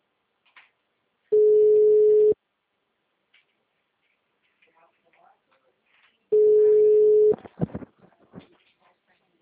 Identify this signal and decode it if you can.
Ringback tone (Europe, 425 Hz), with speech.